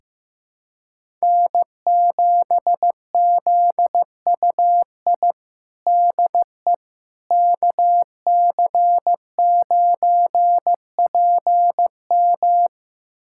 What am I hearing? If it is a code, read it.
Morse code, heard as a keyed tone, about 15 wpm, 700 Hz. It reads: N7ZUI DE KC9PM
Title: USA Callsign DE Callsign - 5 Characters - 15wpm